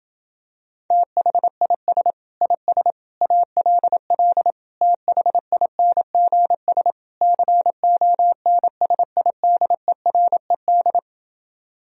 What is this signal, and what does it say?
Morse code, heard as a keyed tone, about 28 wpm, 700 Hz. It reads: THIS IS ALL THINGS CONSIDERED